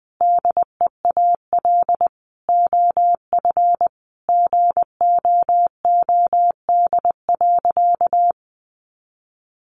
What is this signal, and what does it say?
Morse code, heard as a keyed tone, about 20 wpm, 700 Hz. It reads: DEAL OF GOOD.
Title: Sentences from Top 900 Words 20wpm